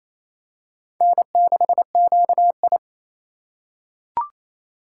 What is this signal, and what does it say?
Morse code, heard as a keyed tone, about 28 wpm, 700 Hz. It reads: N6QI
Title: USA Call Signs - 6x Character Spacing - 28wpm